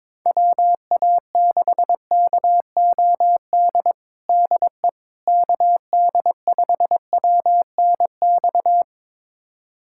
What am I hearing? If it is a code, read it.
Morse code, heard as a keyed tone, about 22 wpm, 700 Hz. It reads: WA6KOD DE KD5WNX
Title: USA Callsign DE Callsign - 6 Characters - 22wpm